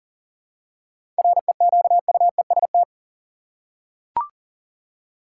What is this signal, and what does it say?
Morse code, heard as a keyed tone, about 40 wpm, 700 Hz. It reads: REQUEST
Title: Seven Letter Words - 40wpm